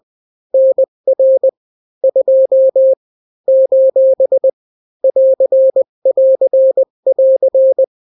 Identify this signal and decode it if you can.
Morse code, heard as a keyed tone, about 20 wpm, 530 Hz. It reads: NR 2 : <AR><AR><AR>
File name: daily / 18wpm